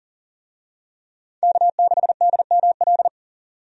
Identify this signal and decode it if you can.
Morse code, heard as a keyed tone, about 40 wpm, 700 Hz. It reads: K6DML
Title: USA Call Signs - 5 Characters - 40wpm